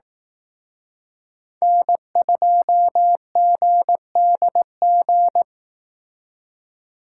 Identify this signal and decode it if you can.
Morse code, heard as a keyed tone, about 18 wpm, 700 Hz. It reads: N2GDG
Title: Speed Racing - USA 1x3 Call Signs 15wpm